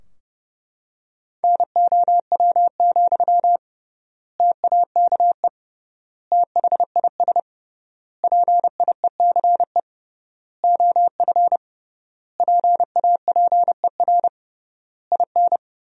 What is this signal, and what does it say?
Morse code, heard as a keyed tone, about 30 wpm, 700 Hz. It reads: NOW, TAKE THIS PIECE OF PAPER IN
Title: 3x Word-Spacing - Sentences from Top 400 Words - 30wpm